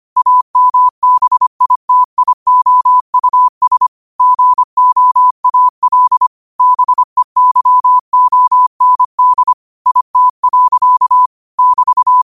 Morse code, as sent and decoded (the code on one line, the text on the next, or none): .- -- -... .. - .. --- ..- ... / --. --- .- .-.. / -... -.-- --- -. -.. / .. - .-.-.- / -...-
AMBITIOUS GOAL BEYOND IT. <BT>